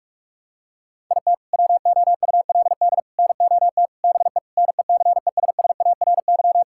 Morse code, heard as a keyed tone, about 47 wpm, 700 Hz. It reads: IT WOULD NOT BE NECESSARY
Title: Sentences from Top 1600 Words 45wpm